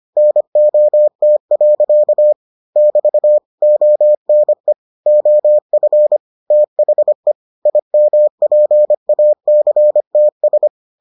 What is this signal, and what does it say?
Morse code, heard as a keyed tone, about 25 wpm, 600 Hz. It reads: NOT. <BT> ONE OF THE IMPACTS